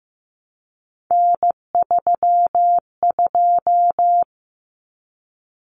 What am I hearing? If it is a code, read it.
Morse code, heard as a keyed tone, about 15 wpm, 700 Hz. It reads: N32
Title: Common Cut Numbers - 3-Digit 15wpm